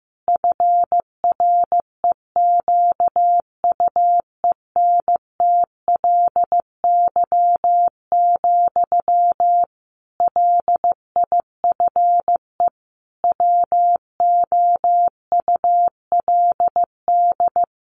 Morse code, heard as a keyed tone, about 15 wpm, 700 Hz. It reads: FREQUENTLY, LIFE WOULD
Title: Sentences from Top 1300 Words 15wpm